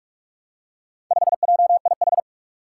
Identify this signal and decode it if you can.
Morse code, heard as a keyed tone, about 45 wpm, 700 Hz. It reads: 5JIH
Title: Four Challenging Letter-Number Combinations 45wpm